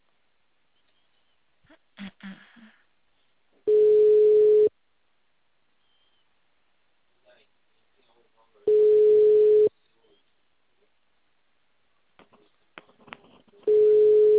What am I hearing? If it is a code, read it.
Ringback tone (Europe, 425 Hz), with speech.